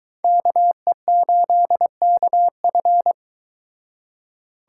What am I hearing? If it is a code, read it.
Morse code, heard as a keyed tone, about 22 wpm, 700 Hz. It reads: KE8KF